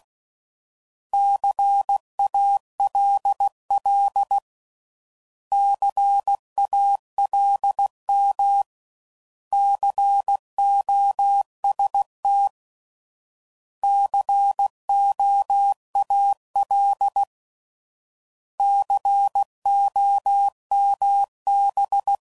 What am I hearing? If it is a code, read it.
Morse code, heard as a keyed tone, about 16 wpm, 800 Hz. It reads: CALL CALM COST COAL COMB
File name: Prefixes / By First Letter / Cs